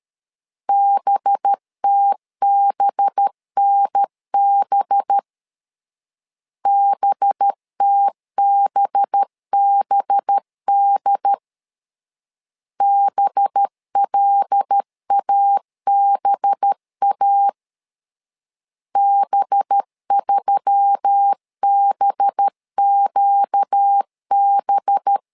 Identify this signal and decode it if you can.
Morse code, heard as a keyed tone, about 13 wpm, 800 Hz. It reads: BTBNB BTBBD BLABA B3BQB
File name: LEZIONI / 5 LBVC / 5 LB 12wpm 1906